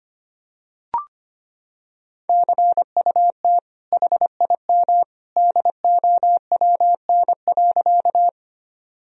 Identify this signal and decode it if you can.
Morse code, heard as a keyed tone, about 25 wpm, 700 Hz. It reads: CUT HIM DOWN.
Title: Sentences from Top 1000 Words - Review Part 1 - 25wpm